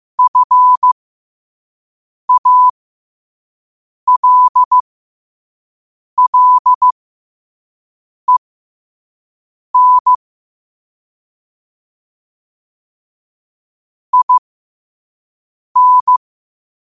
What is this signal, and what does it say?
Morse code, heard as a keyed tone, about 15 wpm, spaced as for 5 wpm, 1000 Hz. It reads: FALLEN IN
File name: qotdfiles / 5wpm